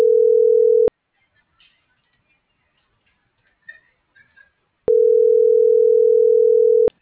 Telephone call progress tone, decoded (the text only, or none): ringback tone